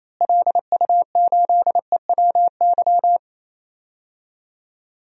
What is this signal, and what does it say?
Morse code, heard as a keyed tone, about 28 wpm, 700 Hz. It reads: LU8EWY
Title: DXCC-100 Call Signs - 6 Characters - Repeated 2x - 28wpm